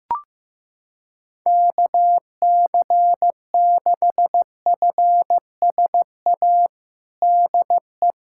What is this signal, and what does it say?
Morse code, heard as a keyed tone, about 15 wpm, 700 Hz. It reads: KC6FSA DE
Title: USA Callsign DE Callsign 15wpm